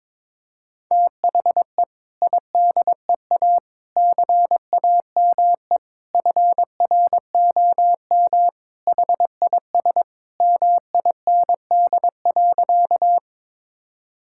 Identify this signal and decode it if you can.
Morse code, heard as a keyed tone, about 22 wpm, 700 Hz. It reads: THE IDEA CAME FROM HIS MIND.
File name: practice / 22wpm / Sentences with 2-Word Phrases - 22wpm